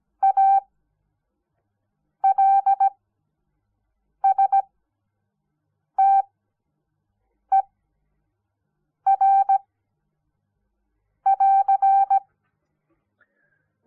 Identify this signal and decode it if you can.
Morse code, heard as a keyed tone, about 17 wpm, spaced as for 5 wpm, 780 Hz. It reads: ALSTER <AR>
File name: Onlinelehrgang / morsen / DJ4UF Morsen06